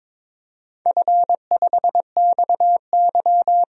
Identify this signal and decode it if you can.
Morse code, heard as a keyed tone, about 22 wpm, 700 Hz. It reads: F5XY
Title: DXCC-100 Call Signs - 4 Characters - 22wpm